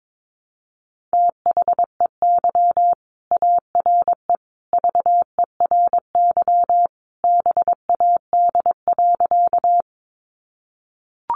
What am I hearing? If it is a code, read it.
Morse code, heard as a keyed tone, about 22 wpm, 700 Hz. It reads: THEY ARE VERY BAD.
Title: Sentences from Top 700 Words 22wpm